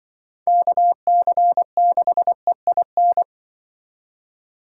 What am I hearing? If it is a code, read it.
Morse code, heard as a keyed tone, about 25 wpm, 700 Hz. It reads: KC6EIN